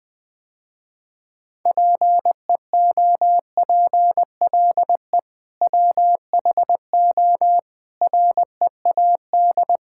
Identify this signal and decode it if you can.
Morse code, heard as a keyed tone, about 20 wpm, 700 Hz. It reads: PEOPLE WHO READ
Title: Sentences with 2-Word Phrases - Spoken Beginning - 20wpm